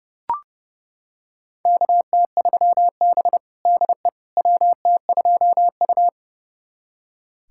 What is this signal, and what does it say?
Morse code, heard as a keyed tone, about 30 wpm, 700 Hz. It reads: KT3B DE WT2U
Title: USA Callsign DE Callsign - 4 Characters - 30wpm